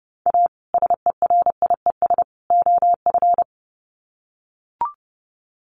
Morse code, heard as a keyed tone, about 30 wpm, 700 Hz. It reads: A SERIES OF
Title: Sets of 3 Words 30wpm